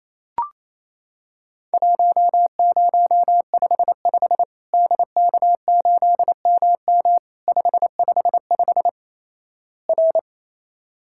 Morse code, heard as a keyed tone, about 28 wpm, 700 Hz. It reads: 1055 DK8MM 555 R
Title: Worked All Europe DX Contest - QTC Exchange - 28wpm